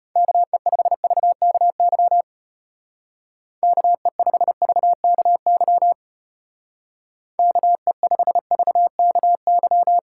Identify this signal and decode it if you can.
Morse code, heard as a keyed tone, about 31 wpm, 700 Hz. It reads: KE5VKY KE5VKY KE5VKY
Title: Speed Racing - USA 2x3 Call Signs 25wpm